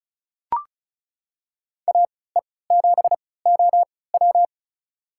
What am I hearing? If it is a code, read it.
Morse code, heard as a keyed tone, about 34 wpm, spaced as for 20 wpm, 700 Hz. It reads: AE7OW
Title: USA Call Signs - 3x Character Spacing - 35wpm